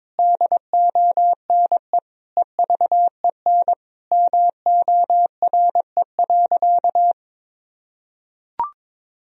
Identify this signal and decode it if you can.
Morse code, heard as a keyed tone, about 22 wpm, 700 Hz. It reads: DONE EVEN MORE.
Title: Sentences from Top 1000 Words - Review Part 1 - 22wpm